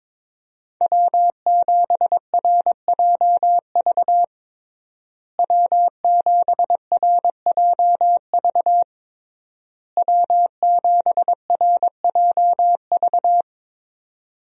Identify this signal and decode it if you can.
Morse code, heard as a keyed tone, about 22 wpm, 700 Hz. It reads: W7RJV W7RJV W7RJV
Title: USA Call Signs - 5 Characters - Repeated 3x - 22wpm